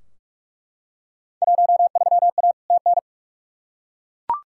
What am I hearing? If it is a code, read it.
Morse code, heard as a keyed tone, about 41 wpm, 700 Hz. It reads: 13A TN
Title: Field Day Exchanges Encore - 45wpm